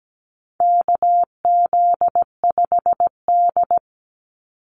Morse code, heard as a keyed tone, about 17 wpm, 700 Hz. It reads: KZ5D
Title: Call Signs Repeated 2x - 17wpm